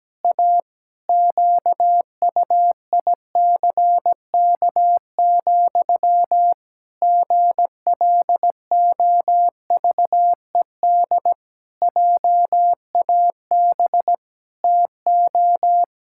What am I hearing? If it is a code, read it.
Morse code, heard as a keyed tone, about 17 wpm, 700 Hz. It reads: A QUICK, GLOVED JAB TO
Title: Pangram Challenge v2 - 17wpm